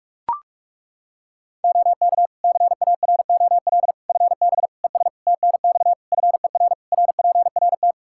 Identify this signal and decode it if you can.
Morse code, heard as a keyed tone, about 41 wpm, 700 Hz. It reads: OK CAROL FB ES TNX FER RPRT
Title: K1USN SST QSO Practice 45wpm